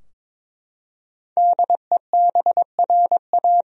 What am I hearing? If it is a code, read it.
Morse code, heard as a keyed tone, about 22 wpm, 700 Hz. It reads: DEBRA
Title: USA Common Names - 22wpm